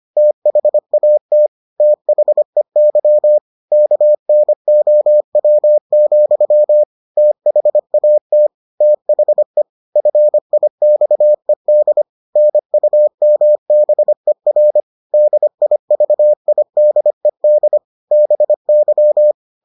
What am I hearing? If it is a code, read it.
Morse code, heard as a keyed tone, about 25 wpm, 600 Hz. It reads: THAT THEY KNOW, THAT THE FIXED NUMBER DIVIDED BY